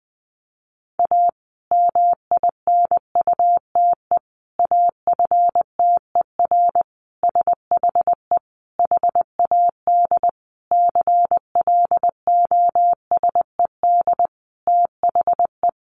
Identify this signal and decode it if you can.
Morse code, heard as a keyed tone, about 19 wpm, 700 Hz. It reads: A MINUTE AFTER SHE HAD CLOSED THE
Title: Sentences from Top 1400 Words 20wpm